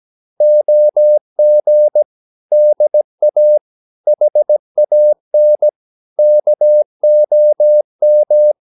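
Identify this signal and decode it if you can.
Morse code, heard as a keyed tone, about 17 wpm, 600 Hz. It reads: OG DA HAN KOM
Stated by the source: Slow (QRS) Morse practice at 17 wpm, from CW med Gnister.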